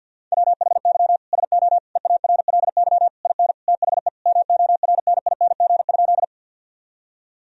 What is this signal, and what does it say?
Morse code, heard as a keyed tone, about 51 wpm, 700 Hz. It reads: WHY SO EARLY IN THE MORNING?
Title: Sentences from Top 1000 Words - Review Part 2 - 50wpm